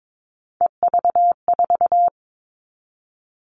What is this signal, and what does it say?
Morse code, heard as a keyed tone, about 22 wpm, 700 Hz. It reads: EV4
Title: All Cut Numbers - 3-Digit 22wpm